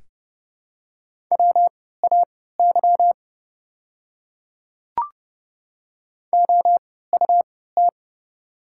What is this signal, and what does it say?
Morse code, heard as a keyed tone, about 30 wpm, spaced as for 17 wpm, 700 Hz. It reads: WAY OUT
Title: 3x Character Spacing - Top 200 Words - 30wpm